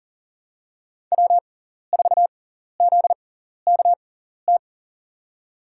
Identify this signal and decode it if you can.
Morse code, heard as a keyed tone, about 40 wpm, spaced as for 14 wpm, 700 Hz. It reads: W4ZKT
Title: USA Call Signs - 5 Characters - 6x Character Spacing - 40wpm